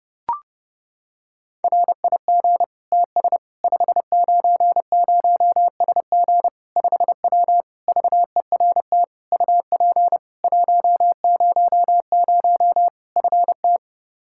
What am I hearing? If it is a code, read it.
Morse code, heard as a keyed tone, about 32 wpm, 700 Hz. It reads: RIG TS 590SG 5W VERT UP 100 FT
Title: Practice QSO - Exchange 2 - Info - 30wpm